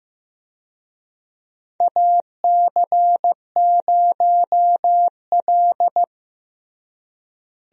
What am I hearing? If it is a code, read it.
Morse code, heard as a keyed tone, about 15 wpm, 700 Hz. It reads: AC0L
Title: USA Call Signs - 4 Characters - Repeated 3x - 15wpm